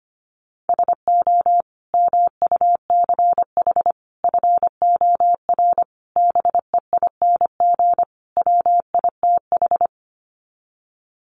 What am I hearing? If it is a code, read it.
Morse code, heard as a keyed tone, about 25 wpm, 700 Hz. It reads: SO MUCH FOR BEING WITH
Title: Sets of 5 Words Encore 25wpm